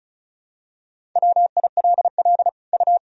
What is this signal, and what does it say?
Morse code, heard as a keyed tone, about 36 wpm, 700 Hz. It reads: WILL U
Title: Sentences from Top 400 Words and Abbreviations 35wpm